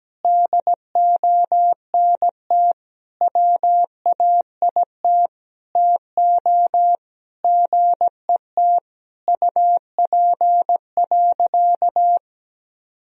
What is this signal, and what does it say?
Morse code, heard as a keyed tone, about 17 wpm, 700 Hz. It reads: DONT WAIT TO GET UP.
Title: Sentences from Top 500 Words and Abbreviations 17wpm